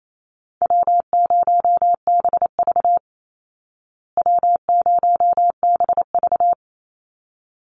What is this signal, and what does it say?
Morse code, heard as a keyed tone, about 28 wpm, 700 Hz. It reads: W0BV W0BV
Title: Easy Call Signs Repeated 2x - 28wpm